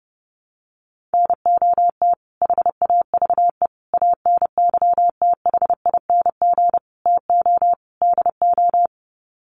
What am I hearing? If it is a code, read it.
Morse code, heard as a keyed tone, about 30 wpm, 700 Hz. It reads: NOT HAVE ANYTHING TO DO
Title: Sets of 5 Words Encore 30wpm